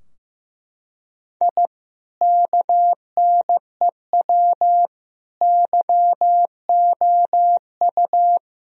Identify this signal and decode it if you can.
Morse code, heard as a keyed tone, about 16 wpm, 700 Hz. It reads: I KNEW YOU
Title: Sentences from Top 1000 Words - Review Part 2 - 15wpm